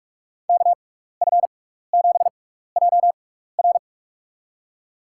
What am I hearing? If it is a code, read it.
Morse code, heard as a keyed tone, about 44 wpm, spaced as for 15 wpm, 700 Hz. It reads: KF7JR
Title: USA Call Signs - 5 Characters - 6x Character Spacing - 45wpm